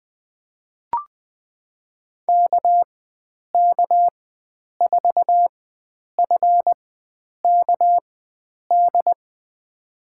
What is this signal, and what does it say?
Morse code, heard as a keyed tone, about 21 wpm, spaced as for 9 wpm, 700 Hz. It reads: KK4FKD
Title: USA Call Signs - 4x Character Spacing - 20wpm